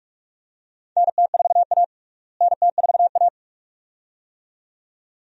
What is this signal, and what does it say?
Morse code, heard as a keyed tone, about 46 wpm, 700 Hz. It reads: NT4A NT4A
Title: USA Call Signs - 4 Characters - Repeated 2x - 45wpm